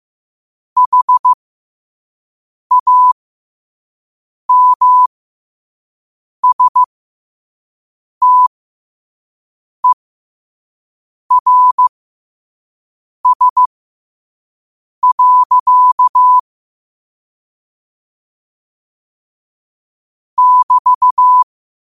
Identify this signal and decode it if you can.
Morse code, heard as a keyed tone, about 15 wpm, spaced as for 5 wpm, 1000 Hz. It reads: HAMSTERS. <BT>
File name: qotdfiles / 5wpm